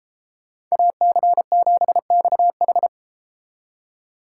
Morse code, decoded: AC7XH

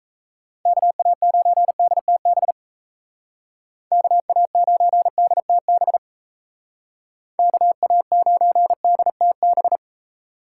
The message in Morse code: KA9DTB KA9DTB KA9DTB

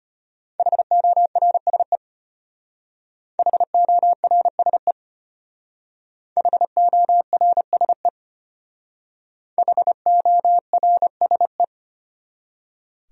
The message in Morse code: HORSE HORSE HORSE HORSE

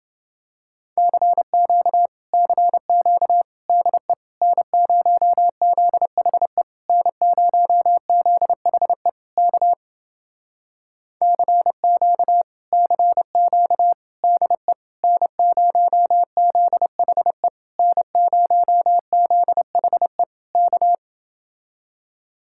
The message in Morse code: CQ CQ DE N0ZHE N0ZHE K CQ CQ DE N0ZHE N0ZHE K

